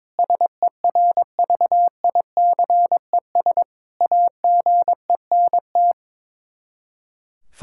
Morse code, heard as a keyed tone, about 22 wpm, 700 Hz. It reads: SERVICES AGENT
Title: K1USN SST QSO Practice 22wpm